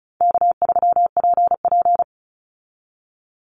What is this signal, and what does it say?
Morse code, heard as a keyed tone, about 34 wpm, 700 Hz. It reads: K3PP
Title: Easy Call Signs Repeated 3x - 35wpm